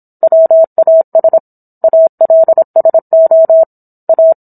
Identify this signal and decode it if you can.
Morse code, heard as a keyed tone, about 27 wpm, 650 Hz. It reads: WAS ALSO A